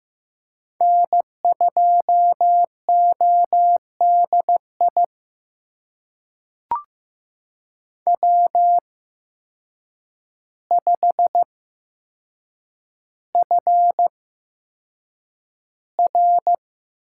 Morse code, heard as a keyed tone, about 15 wpm, spaced as for 7 wpm, 700 Hz. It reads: N2ODI W5FR
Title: USA Call Signs - 8x Character Spacing - 15wpm